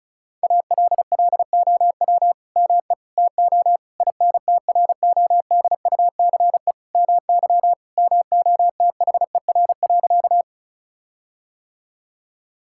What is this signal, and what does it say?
Morse code, heard as a keyed tone, about 36 wpm, 700 Hz. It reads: ALLOW ME TO INTRODUCE MY MOTHER.
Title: Sentences from Top 1600 Words 35wpm